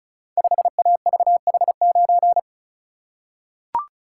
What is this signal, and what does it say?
Morse code, heard as a keyed tone, about 36 wpm, 700 Hz. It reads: 5AVH9
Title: Code Groups - 5 Letter:Numbers - 35wpm